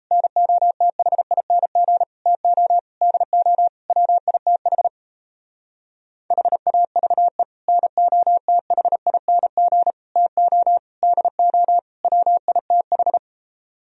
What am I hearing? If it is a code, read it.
Morse code, heard as a keyed tone, about 36 wpm, 700 Hz. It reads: NOTHING TO DO WITH HAVE NOTHING TO DO WITH